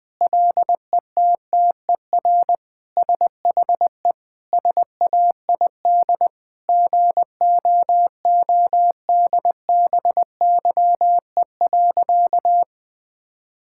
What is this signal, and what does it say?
Morse code, heard as a keyed tone, about 20 wpm, 700 Hz. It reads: LETTER SHE SAID GOODBYE.
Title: Sentences with 4-Word Phrases - 20wpm